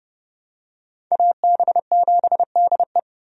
Morse code, heard as a keyed tone, about 31 wpm, 700 Hz. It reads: AB7DE
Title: USA Call Signs - 5 Characters - Repeated 3x - 30wpm